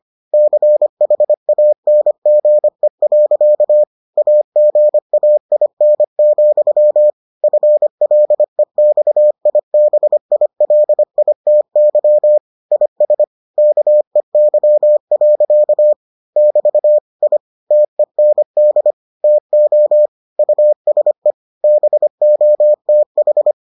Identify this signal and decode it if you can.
Morse code, heard as a keyed tone, about 25 wpm, 600 Hz. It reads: CHANGE. AGAIN, FLEXIBILITY IS KEY. <BT> I TEND TO USE BOTH